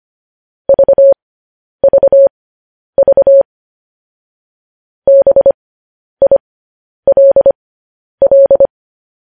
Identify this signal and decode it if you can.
Morse code, heard as a keyed tone, about 25 wpm, spaced as for 10 wpm, 550 Hz. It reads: VVV BILL
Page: news-headlines-in-morse-code-at-10-wpm